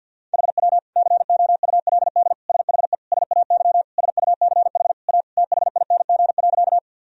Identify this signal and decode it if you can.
Morse code, heard as a keyed tone, about 52 wpm, 700 Hz. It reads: HW COULD SHE SAY SUCH A THING.
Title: Sentences from Top 200 Words and Abbreviations 50wpm